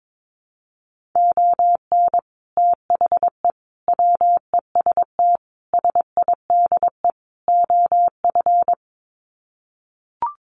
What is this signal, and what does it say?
Morse code, heard as a keyed tone, about 22 wpm, 700 Hz. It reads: ON THE WEST SIDE OF
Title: Sets of 5 Words Encore2 22wpm